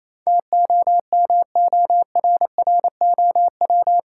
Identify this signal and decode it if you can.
Morse code, heard as a keyed tone, about 28 wpm, 700 Hz. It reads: TOMORROW
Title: Sentences from Top 1100 Words - New Word Drill Alt - 28wpm